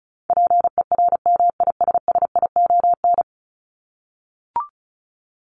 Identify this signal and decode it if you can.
Morse code, heard as a keyed tone, about 34 wpm, 700 Hz. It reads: PERMISSION
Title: Top 1501-1750 Words 35wpm